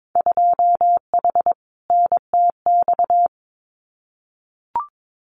Morse code, heard as a keyed tone, about 22 wpm, 700 Hz. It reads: 2H NTX
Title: Winter Field Day Exchanges Encore 22wpm